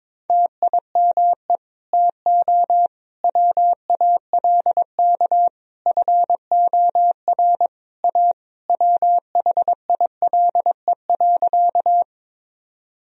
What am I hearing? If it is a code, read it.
Morse code, heard as a keyed tone, about 22 wpm, 700 Hz. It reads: TIME TO WALK FOR A WHILE.
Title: Sentences from Top 1000 Words - Review Part 1 - 22wpm